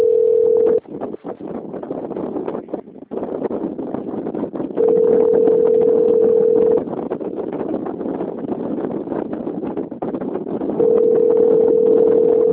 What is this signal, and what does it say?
Telephone call progress tone (North America), heard as a telephone network's tone, about 440, 480 Hz, with noise. Ringback tone.